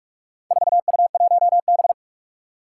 Morse code, heard as a keyed tone, about 41 wpm, 700 Hz. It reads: 4U1B
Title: Four Challenging Letter-Number Combinations 45wpm